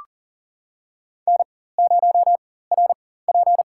Morse code, heard as a keyed tone, about 40 wpm, spaced as for 19 wpm, 700 Hz. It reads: N0RP
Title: USA Call Signs - 4 Characters - 4x Character Spacing - 40wpm